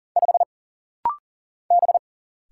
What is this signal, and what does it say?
Morse code, heard as a keyed tone, about 40 wpm, 700 Hz. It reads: 5 B
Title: Single Letter-Number Alt - Rapid-Fire 40wpm